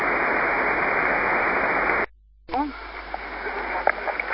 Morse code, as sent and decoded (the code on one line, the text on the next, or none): none